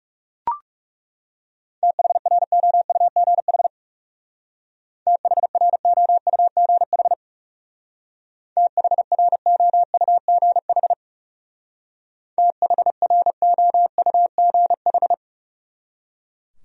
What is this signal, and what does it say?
Morse code, heard as a keyed tone, about 37 wpm, 700 Hz. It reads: THROUGH THROUGH THROUGH THROUGH
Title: Speed-Racing Top 300 Words 30wpm